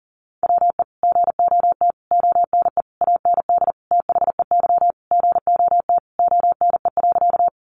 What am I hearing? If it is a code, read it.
Morse code, heard as a keyed tone, about 40 wpm, 700 Hz. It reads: WE GOT ONE AND THEY GOT ONE.
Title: Sentences from Top 300 Words 40wpm